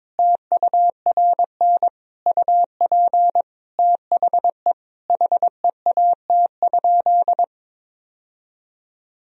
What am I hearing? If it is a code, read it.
Morse code, heard as a keyed tone, about 22 wpm, 700 Hz. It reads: TURN UP THE HEAT?
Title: Sentences from Top 1000 Words - Review Part 3 - 22wpm